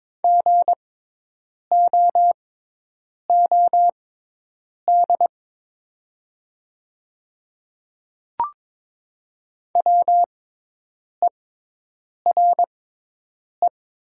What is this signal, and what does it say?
Morse code, heard as a keyed tone, about 22 wpm, spaced as for 8 wpm, 700 Hz. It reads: GOOD WERE
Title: 6x Character Spacing - Top 200 Words - 22wpm